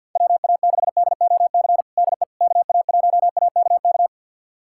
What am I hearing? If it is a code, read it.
Morse code, heard as a keyed tone, about 52 wpm, 690 Hz. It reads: WA6DOC DE KA1AKK
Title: USA Callsign DE Callsign - 6 Characters - 50wpm